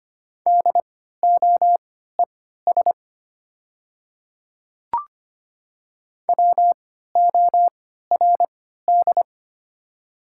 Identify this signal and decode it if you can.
Morse code, heard as a keyed tone, about 25 wpm, spaced as for 14 wpm, 700 Hz. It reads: DOES WORD
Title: 3x Character Spacing - Top 300 Words - 25wpm